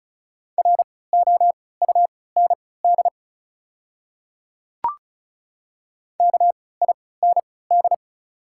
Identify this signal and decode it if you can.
Morse code, heard as a keyed tone, about 36 wpm, spaced as for 20 wpm, 700 Hz. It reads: ROUND KIND
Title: 3x Character Spacing - Top 200 Words - 35wpm